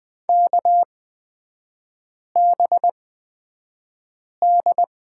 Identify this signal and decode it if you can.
Morse code, heard as a keyed tone, about 20 wpm, spaced as for 5 wpm, 700 Hz. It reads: KBD
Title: 255 Lesson 41e - Period - Three Characters 20wpm